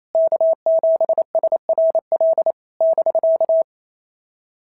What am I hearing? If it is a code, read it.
Morse code, heard as a keyed tone, about 28 wpm, 650 Hz. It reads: K7SRL <BK>
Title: SOTA QSO Practice 28wpm